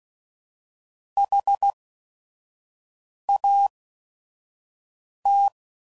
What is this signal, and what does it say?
Morse code, heard as a keyed tone, about 16 wpm, spaced as for 5 wpm, 800 Hz. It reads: HAT
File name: CW Code Groups @5WPM / QSOs / QSO WD0FBT